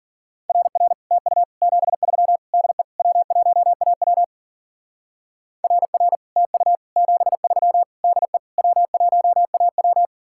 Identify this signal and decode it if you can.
Morse code, heard as a keyed tone, about 48 wpm, 700 Hz. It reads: RR TU 73 DE W1AW RR TU 73 DE W1AW
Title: Speed-Racing Common QSO Phrases Encore - 40wpm